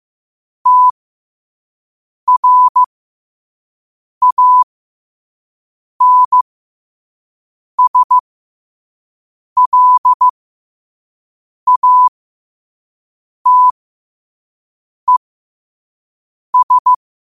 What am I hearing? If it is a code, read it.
Morse code, heard as a keyed tone, about 15 wpm, spaced as for 5 wpm, 1000 Hz. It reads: TRANSLATES
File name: qotdfiles / 5wpm